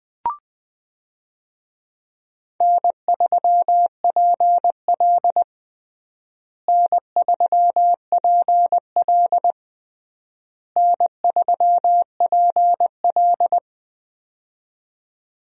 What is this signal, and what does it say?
Morse code, heard as a keyed tone, about 20 wpm, 700 Hz. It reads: N3PL N3PL N3PL
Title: USA Call Signs - 4 Characters - Repeated 3x - 20wpm